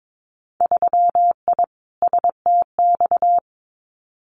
Morse code, heard as a keyed tone, about 22 wpm, 700 Hz. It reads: 3I STX
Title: Winter Field Day Exchanges Encore 22wpm